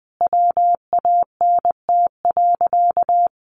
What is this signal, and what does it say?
Morse code, heard as a keyed tone, about 20 wpm, 700 Hz. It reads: WANT.
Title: Sentences from Top 1100 Words 20wpm